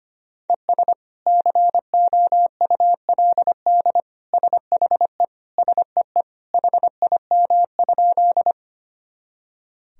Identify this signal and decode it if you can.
Morse code, heard as a keyed tone, about 25 wpm, 700 Hz. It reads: ES COULD SHE SEE HIM?
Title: Sentences from Top 100 Words and Abbreviations 25wpm